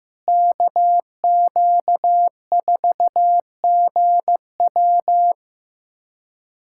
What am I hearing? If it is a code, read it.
Morse code, heard as a keyed tone, about 15 wpm, 700 Hz. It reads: KQ4GW
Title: USA Callsign DE Callsign - 5 Characters - 15wpm